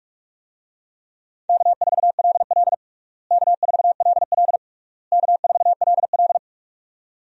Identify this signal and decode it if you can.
Morse code, heard as a keyed tone, about 41 wpm, 700 Hz. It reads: K4LL K4LL K4LL
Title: USA Call Signs - 4 Characters - Repeated 3x - 45wpm